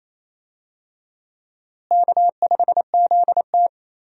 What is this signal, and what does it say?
Morse code, heard as a keyed tone, about 28 wpm, 700 Hz. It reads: K5ZT
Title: USA Call Signs - 4 Characters - Repeated 2x - 28wpm